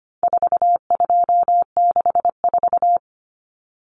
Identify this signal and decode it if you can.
Morse code, heard as a keyed tone, about 25 wpm, 700 Hz. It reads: 4264
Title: Common Cut Numbers - 4-Digit 25wpm